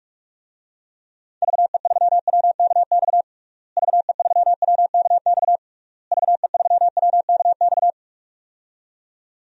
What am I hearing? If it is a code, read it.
Morse code, heard as a keyed tone, about 46 wpm, 700 Hz. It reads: VE3WKX VE3WKX VE3WKX
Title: DXCC-100 Call Signs - 6 Characters - Repeated 3x - 45wpm